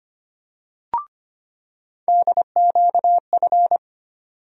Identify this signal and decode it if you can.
Morse code, heard as a keyed tone, about 25 wpm, 700 Hz. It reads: DQF
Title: USA Call Sign Postfixes - 3 Letters 25wpm